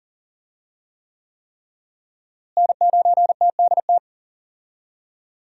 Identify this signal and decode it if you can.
Morse code, heard as a keyed tone, about 40 wpm, 700 Hz. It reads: N9TDT
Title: USA Call Signs - 5 Characters - Repeated 3x - 40wpm